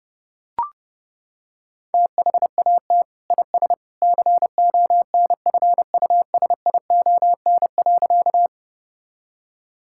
Morse code, heard as a keyed tone, about 28 wpm, 700 Hz. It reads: THAT IS CONFUSION.
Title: Sentences from Top 1600 Words 30wpm